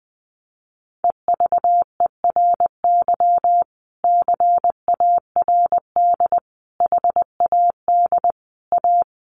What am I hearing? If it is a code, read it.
Morse code, heard as a keyed tone, about 20 wpm, 700 Hz. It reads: EVERY CARD HAD A
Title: Sentences from Top 1400 Words 20wpm